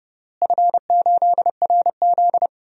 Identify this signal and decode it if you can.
Morse code, heard as a keyed tone, about 30 wpm, 700 Hz. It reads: F8RZ
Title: DXCC-100 Call Signs - 4 Characters - Repeated 3x - 30wpm